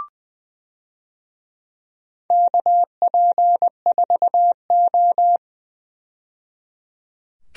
Morse code, heard as a keyed tone, about 20 wpm, 700 Hz. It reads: KP4O
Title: DXCC-100 Call Signs - 4 Characters - 20wpm